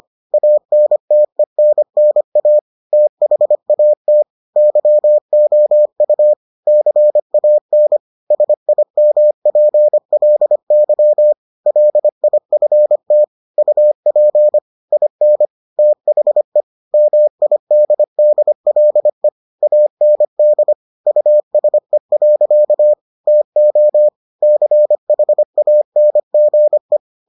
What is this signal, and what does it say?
Morse code, heard as a keyed tone, about 25 wpm, 600 Hz. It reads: ANTENNA THAT YOU CAN SIMPLY LIFT UP IN THE MIDDLE AND USE. TO CHANGE